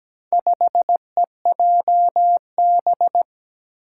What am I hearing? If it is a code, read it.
Morse code, heard as a keyed tone, about 17 wpm, 700 Hz. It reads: 5EJB
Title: Four Challenging Letter-Number Combinations 17wpm